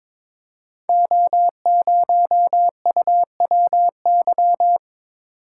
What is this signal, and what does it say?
Morse code, heard as a keyed tone, about 22 wpm, 700 Hz. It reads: O0UWY
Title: Code Groups - 5 Letter:Numbers - 22wpm